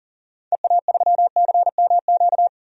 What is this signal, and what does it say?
Morse code, heard as a keyed tone, about 40 wpm, 700 Hz. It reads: EA3CMQ
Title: DXCC-100 Call Signs - 6 Characters - 40wpm